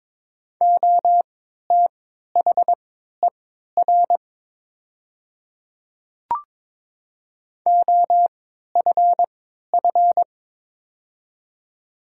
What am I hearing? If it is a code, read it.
Morse code, heard as a keyed tone, about 22 wpm, spaced as for 12 wpm, 690 Hz. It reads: OTHER OFF
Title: 3x Character Spacing - Top 200 Words - 22wpm